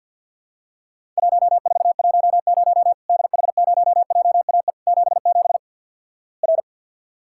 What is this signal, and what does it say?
Morse code, heard as a keyed tone, about 50 wpm, 700 Hz. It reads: 1410 DH0JAE 77 R
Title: Worked All Europe DX Contest - QTC Exchange with Repeats - 50wpm